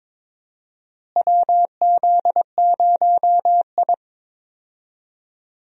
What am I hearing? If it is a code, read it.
Morse code, heard as a keyed tone, about 22 wpm, 700 Hz. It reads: WZ0I
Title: USA Call Signs - 4 Characters - Repeated 3x - 22wpm